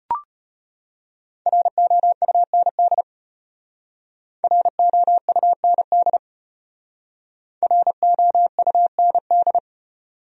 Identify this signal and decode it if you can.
Morse code, heard as a keyed tone, about 34 wpm, 700 Hz. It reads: ROUND ROUND ROUND